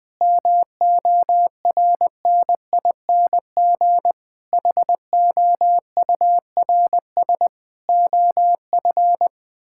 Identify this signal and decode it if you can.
Morse code, heard as a keyed tone, about 20 wpm, 700 Hz. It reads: MORNING HOURS OF
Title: Sets of 6 Words - 20wpm